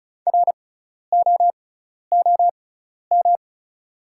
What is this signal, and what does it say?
Morse code, heard as a keyed tone, about 34 wpm, spaced as for 12 wpm, 700 Hz. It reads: ROOM
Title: 6x Character Spacing - Top 500 Words - 35wpm